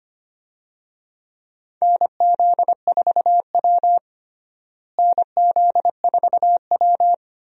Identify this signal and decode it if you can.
Morse code, heard as a keyed tone, about 25 wpm, 700 Hz. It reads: NZ4W NZ4W
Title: USA Call Signs - 4 Characters - Repeated 3x - 25wpm